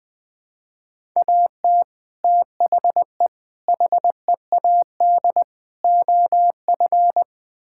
Morse code, heard as a keyed tone, about 20 wpm, 700 Hz. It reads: AT THE HEAD OF
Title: Build A Phrase - 5-Word Phrases - 20wpm